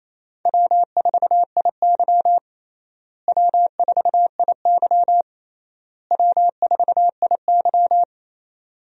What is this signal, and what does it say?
Morse code, heard as a keyed tone, about 28 wpm, 700 Hz. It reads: W4IY W4IY W4IY
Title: USA Call Signs - 4 Characters - Repeated 3x - 28wpm